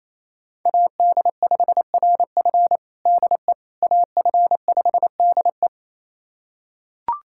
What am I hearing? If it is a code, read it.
Morse code, heard as a keyed tone, about 28 wpm, 700 Hz. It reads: AD5RF DE AF5DE
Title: USA Callsign DE Callsign - 5 Characters - 28wpm